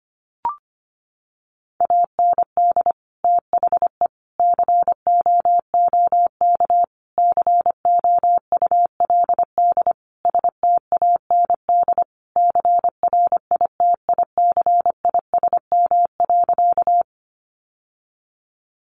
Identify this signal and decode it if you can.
Morse code, heard as a keyed tone, about 25 wpm, 700 Hz. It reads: AND THE COOK COULD STAND CRITICISM.
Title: Sentences from Top 1300 Words 25wpm